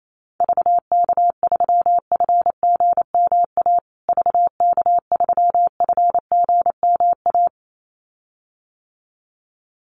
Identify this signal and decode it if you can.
Morse code, heard as a keyed tone, about 28 wpm, 700 Hz. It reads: VK3FGMA VK3FGMA
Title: USA Call Signs Repeated 2x 28wpm